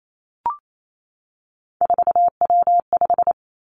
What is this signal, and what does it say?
Morse code, heard as a keyed tone, about 28 wpm, 700 Hz. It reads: 4W5
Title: Three Challenging Letter-Number Combinations 28wpm